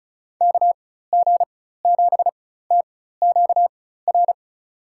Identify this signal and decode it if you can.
Morse code, heard as a keyed tone, about 34 wpm, spaced as for 16 wpm, 700 Hz. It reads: KG7TQR
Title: USA Call Signs - 6 Characters - 4x Character Spacing - 35wpm